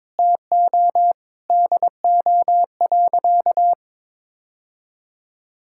Morse code, heard as a keyed tone, about 22 wpm, 700 Hz. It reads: TO DO.
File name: practice / 22wpm / Sentences with 3-Word Phrases - 22wpm